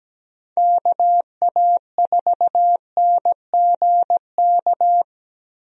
Morse code, heard as a keyed tone, about 17 wpm, 700 Hz. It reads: KA4NGK
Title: USA Call Signs - 6 Characters - Repeated 3x - 17wpm